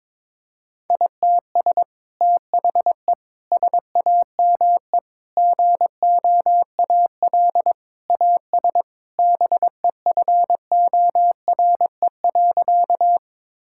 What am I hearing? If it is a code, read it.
Morse code, heard as a keyed tone, about 22 wpm, 700 Hz. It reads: ITS THE SAME GOAL AS BEFORE.
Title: Sentences from Top 1000 Words - Review Part 5 - 22wpm